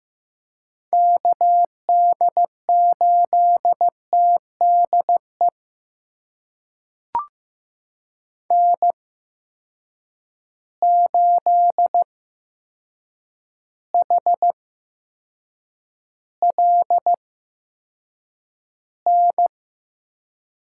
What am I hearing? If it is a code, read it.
Morse code, heard as a keyed tone, about 15 wpm, spaced as for 7 wpm, 700 Hz. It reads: KD8TDE N8HLN
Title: USA Call Signs - 8x Character Spacing - 15wpm